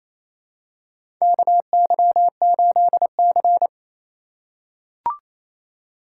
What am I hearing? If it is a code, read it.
Morse code, heard as a keyed tone, about 28 wpm, 700 Hz. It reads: KY8C
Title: USA Call Signs - 4 Characters - 3x Character Spacing - 28wpm